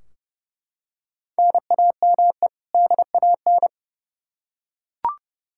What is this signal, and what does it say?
Morse code, heard as a keyed tone, about 32 wpm, 700 Hz. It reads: NAME DAN
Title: Practice QSO - Exchange 1 - Beginning - 30wpm